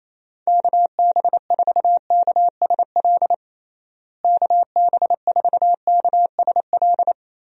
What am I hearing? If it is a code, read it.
Morse code, heard as a keyed tone, about 28 wpm, 700 Hz. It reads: KB4KSL KB4KSL
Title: USA Call Signs - 6 Characters - Repeated 3x - 28wpm